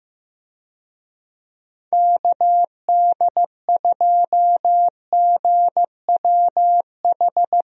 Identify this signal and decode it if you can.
Morse code, heard as a keyed tone, about 15 wpm, 700 Hz. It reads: KD2GWH
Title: USA Call Signs - 6 Characters - Repeated 3x - 15wpm